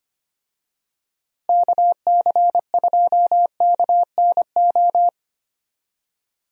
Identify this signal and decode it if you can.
Morse code, heard as a keyed tone, about 25 wpm, 700 Hz. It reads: KC2KNO
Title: USA Call Signs - 6 Characters - Repeated 2x - 25wpm